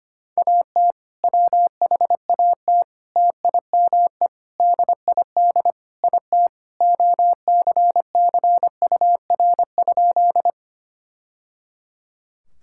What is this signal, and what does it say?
Morse code, heard as a keyed tone, about 25 wpm, 700 Hz. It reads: AT WHAT TIME DID IT OCCUR?
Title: Sentences from Top 1000 Words - Review Part 6 - 25wpm